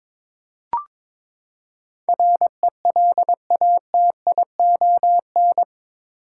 Morse code, heard as a keyed tone, about 22 wpm, 700 Hz. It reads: RELATION
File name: practice / 22wpm / Eight Letter Words - 22wpm